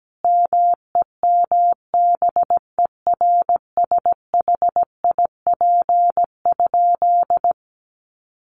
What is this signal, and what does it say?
Morse code, heard as a keyed tone, about 17 wpm, 700 Hz. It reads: MEMBERSHIP?
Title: Sentences from Top 1400 Words 17wpm